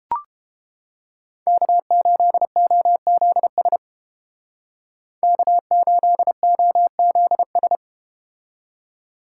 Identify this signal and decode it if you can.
Morse code, heard as a keyed tone, about 31 wpm, 700 Hz. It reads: K8OZS K8OZS